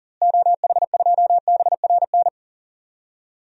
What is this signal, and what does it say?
Morse code, heard as a keyed tone, about 40 wpm, 700 Hz. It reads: OH2BRN
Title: DXCC-100 Call Signs - 6 Characters - Repeated 3x - 40wpm